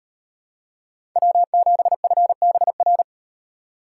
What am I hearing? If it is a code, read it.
Morse code, heard as a keyed tone, about 38 wpm, 700 Hz. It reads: W7FBR